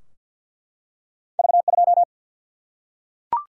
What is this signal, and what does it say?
Morse code, heard as a keyed tone, about 50 wpm, 700 Hz. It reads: V2
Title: Single-Letter Single-Number 50wpm